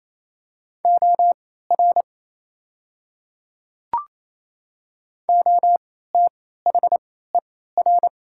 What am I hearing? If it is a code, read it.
Morse code, heard as a keyed tone, about 27 wpm, spaced as for 16 wpm, 690 Hz. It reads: OR OTHER
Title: 3x Character Spacing - Top 100 Words - 28wpm